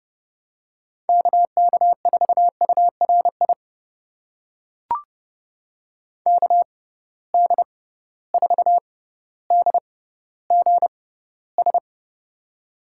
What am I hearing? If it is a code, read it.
Morse code, heard as a keyed tone, about 32 wpm, spaced as for 16 wpm, 700 Hz. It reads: KK4URI KD4DGS
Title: USA Call Signs - 6 Characters - 6x Character Spacing - 30wpm